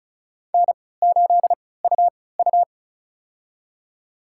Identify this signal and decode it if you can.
Morse code, heard as a keyed tone, about 36 wpm, spaced as for 20 wpm, 700 Hz. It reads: N8UU
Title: USA Call Signs - 4 Characters - 3x Character Spacing - 35wpm